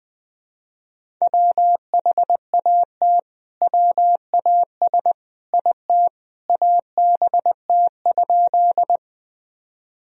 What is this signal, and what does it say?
Morse code, heard as a keyed tone, about 20 wpm, 700 Hz. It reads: WHAT WAS IT ABT?
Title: Sentences from Top 100 Words and Abbreviations 20wpm